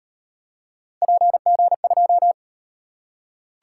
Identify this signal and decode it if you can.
Morse code, heard as a keyed tone, about 35 wpm, 700 Hz. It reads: PG2